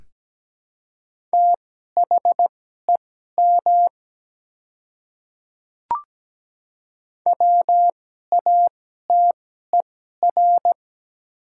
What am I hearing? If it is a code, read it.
Morse code, heard as a keyed tone, about 17 wpm, spaced as for 12 wpm, 700 Hz. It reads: THEM WATER
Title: 2x Character Spacing - Top 100 Words - 17wpm